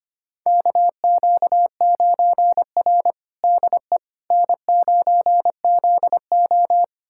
Morse code, heard as a keyed tone, about 25 wpm, 700 Hz. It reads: KQ9R DE N9ZO
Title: USA Callsign DE Callsign - 4 Characters - 25wpm